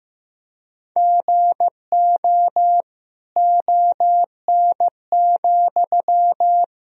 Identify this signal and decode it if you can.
Morse code, heard as a keyed tone, about 15 wpm, 700 Hz. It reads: GO ON,
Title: Sentences from Top 1000 Words - Review Part 4 - 15wpm